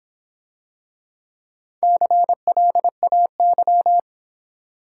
Morse code, heard as a keyed tone, about 26 wpm, 700 Hz. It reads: CLAY